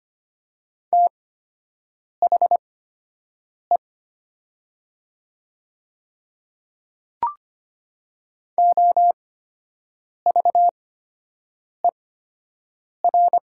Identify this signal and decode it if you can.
Morse code, heard as a keyed tone, about 25 wpm, spaced as for 7 wpm, 700 Hz. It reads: THE OVER
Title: 8x Character Spacing - Top 100 Words - 25wpm